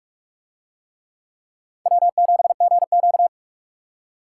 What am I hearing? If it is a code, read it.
Morse code, heard as a keyed tone, about 44 wpm, 700 Hz. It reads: W7GQ